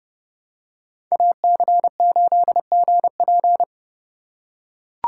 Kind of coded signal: Morse code

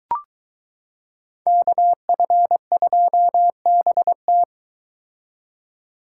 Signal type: Morse code

